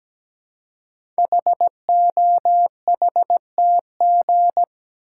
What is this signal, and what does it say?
Morse code, heard as a keyed tone, about 17 wpm, 700 Hz. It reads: HOHTG
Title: Code Groups - 5 Letters - 17wpm